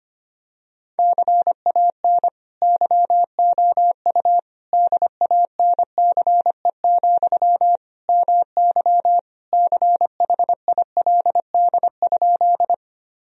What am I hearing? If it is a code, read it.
Morse code, heard as a keyed tone, about 25 wpm, 700 Hz. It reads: CAN YOU DANCE, MY CHILD?
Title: Sentences from Top 1000 Words - Review Part 4 - 25wpm